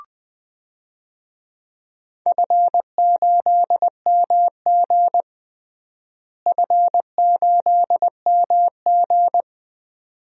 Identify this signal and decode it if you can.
Morse code, heard as a keyed tone, about 20 wpm, 700 Hz. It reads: F8MG F8MG
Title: DXCC-100 Call Signs - 4 Characters - Repeated 2x - 20wpm